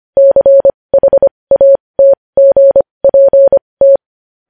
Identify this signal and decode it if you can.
Morse code, heard as a keyed tone, about 25 wpm, spaced as for 20 wpm, 550 Hz. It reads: CHATGPT
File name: podcast / News headlines-1x-25x20